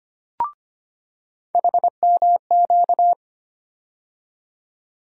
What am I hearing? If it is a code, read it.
Morse code, heard as a keyed tone, about 25 wpm, 700 Hz. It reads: HMQ
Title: USA Call Sign Postfixes - 3 Letters 25wpm